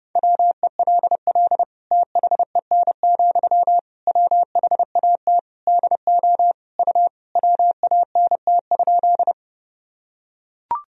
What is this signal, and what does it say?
Morse code, heard as a keyed tone, about 30 wpm, 700 Hz. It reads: WELL THEN, WHAT DO U WANT?
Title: Sentences from Top 200 Words and Abbreviations 30wpm